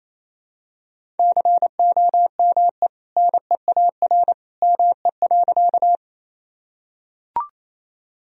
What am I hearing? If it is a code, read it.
Morse code, heard as a keyed tone, about 28 wpm, 700 Hz. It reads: COME NEAR ME.
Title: Sentences from Top 1000 Words - Review Part 1 - 28wpm